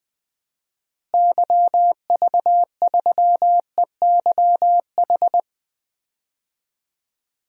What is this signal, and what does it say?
Morse code, heard as a keyed tone, about 20 wpm, 700 Hz. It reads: YV3EYH
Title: DXCC-100 Call Signs - 6 Characters - Repeated 2x - 20wpm